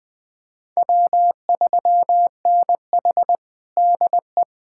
Morse code, heard as a keyed tone, about 21 wpm, 700 Hz. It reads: W3NH DE
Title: USA Callsign DE Callsign - 4 Characters - 20wpm